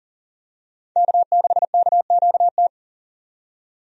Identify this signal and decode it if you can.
Morse code, heard as a keyed tone, about 40 wpm, 700 Hz. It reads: K6KQT
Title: USA Call Signs - 5 Characters - 40wpm